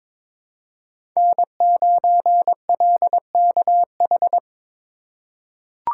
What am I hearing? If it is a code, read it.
Morse code, heard as a keyed tone, about 22 wpm, 700 Hz. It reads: N9LKH
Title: USA Call Signs - 5 Characters - 4x Character Spacing - 22wpm